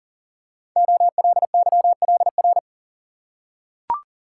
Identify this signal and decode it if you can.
Morse code, heard as a keyed tone, about 42 wpm, 690 Hz. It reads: OLYLR
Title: Code Groups - 5 Letter:Numbers - 40wpm